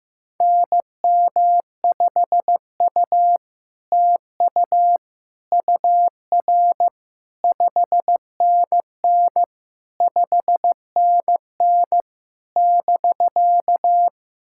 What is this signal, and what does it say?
Morse code, heard as a keyed tone, about 15 wpm, 700 Hz. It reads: NM5U TU UR 5NN 5NN <BK>
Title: SOTA QSO Practice 15wpm